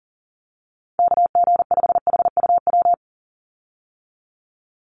Morse code, heard as a keyed tone, about 40 wpm, 700 Hz. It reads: KG5HUW
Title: USA Call Signs 40wpm